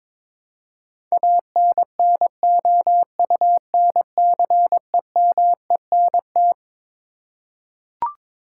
Morse code, heard as a keyed tone, about 21 wpm, 700 Hz. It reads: ANNOUNCEMENT
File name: Twelve Letter Words - 22wpm